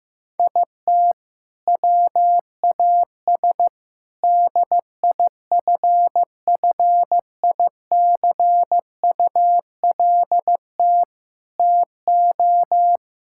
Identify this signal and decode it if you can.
Morse code, heard as a keyed tone, about 15 wpm, 700 Hz. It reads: IT WAS DIFFICULT TO
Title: Sentences from Top 1000 Words - Review Part 5 - 15wpm